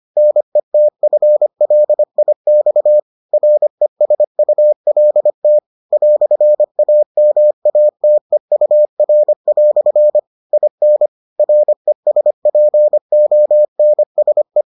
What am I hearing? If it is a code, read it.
Morse code, heard as a keyed tone, about 25 wpm, 600 Hz. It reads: NETFLIX RESULT "AMATEUR" IN RESPONSE